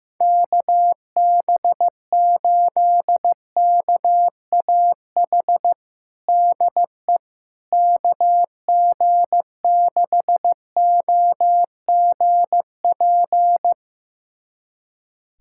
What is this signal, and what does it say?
Morse code, heard as a keyed tone, about 16 wpm, 700 Hz. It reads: KB8KAH DE KG6OGP
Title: USA Callsign DE Callsign - 6 Characters - 15wpm